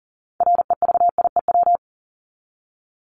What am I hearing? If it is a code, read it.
Morse code, heard as a keyed tone, about 40 wpm, 700 Hz. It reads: REVIEW
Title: Top 751-1000 Words 40wpm